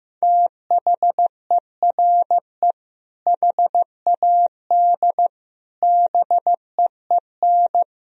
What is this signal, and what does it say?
Morse code, heard as a keyed tone, about 15 wpm, 700 Hz. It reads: THERE HAD BEEN
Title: Sentences from Top 1000 Words - Review Part 1 - 15wpm